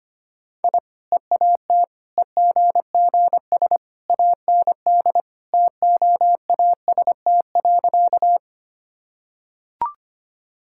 Morse code, heard as a keyed tone, about 25 wpm, 700 Hz. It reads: I EAT EGGS AND TOAST.
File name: Easy Sentences with Hints - 25wpm